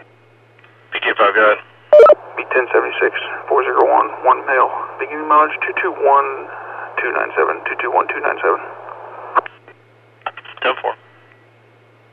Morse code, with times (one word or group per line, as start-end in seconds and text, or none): none